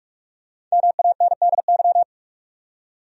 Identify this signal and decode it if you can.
Morse code, heard as a keyed tone, about 41 wpm, 700 Hz. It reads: MANDY
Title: USA Common Names - 45wpm